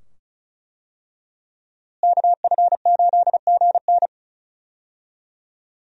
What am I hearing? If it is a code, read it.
Morse code, heard as a keyed tone, about 35 wpm, 700 Hz. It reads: KF8GN
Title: USA Call Signs - 5 Characters - Repeated 2x - 35wpm